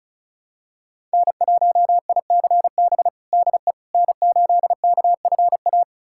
Morse code, heard as a keyed tone, about 36 wpm, 700 Hz. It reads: N1ICB DE N8KFA
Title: USA Callsign DE Callsign - 5 Characters - 35wpm